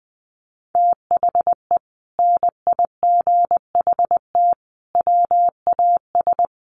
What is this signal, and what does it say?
Morse code, heard as a keyed tone, about 20 wpm, 700 Hz. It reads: THE NIGHT WAS
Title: Sentences from Top 1200 Words 20wpm